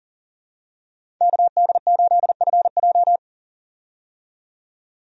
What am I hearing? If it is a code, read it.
Morse code, heard as a keyed tone, about 40 wpm, 700 Hz. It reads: KD8FJ
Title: USA Call Signs - 5 Characters - 40wpm